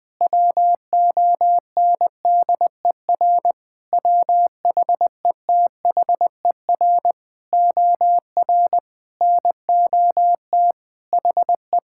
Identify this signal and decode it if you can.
Morse code, heard as a keyed tone, about 20 wpm, 700 Hz. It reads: WONDER WHETHER OR NOT HE
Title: Sentences with 3-Word Phrases - Spoken Beginning - 20wpm